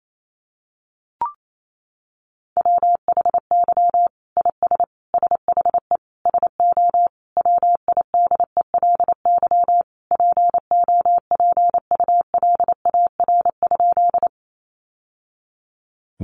Morse code, heard as a keyed tone, about 28 wpm, 700 Hz. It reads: WHY IS SHE SO WIDELY POPULAR?
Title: Sentences from Top 1200 Words 28wpm